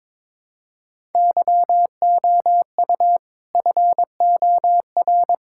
Morse code, heard as a keyed tone, about 22 wpm, 700 Hz. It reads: YOU FOR
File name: practice / 22wpm / Build A Phrase - 5-Word Phrases - 22wpm